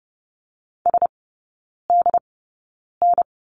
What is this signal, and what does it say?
Morse code, heard as a keyed tone, about 30 wpm, spaced as for 9 wpm, 700 Hz. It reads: SDN
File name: course / 30:10wpm / 087 Lesson 15e - Number 5 - Three Characters 30wpm